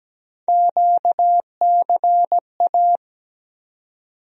Morse code, heard as a keyed tone, about 17 wpm, 700 Hz. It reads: QCA
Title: USA Call Sign Postfixes - 3 Letters 17wpm